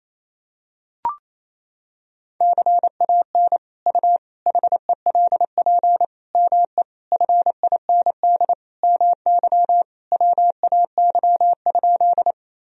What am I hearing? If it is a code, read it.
Morse code, heard as a keyed tone, about 28 wpm, 700 Hz. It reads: CAN U HELP ME FIND MY WAY?
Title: Sentences from Top 200 Words and Abbreviations 28wpm